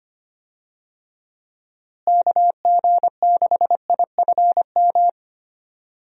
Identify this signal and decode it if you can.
Morse code, heard as a keyed tone, about 25 wpm, 700 Hz. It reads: KG6IFM
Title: USA Call Signs - 6 Characters - Repeated 2x - 25wpm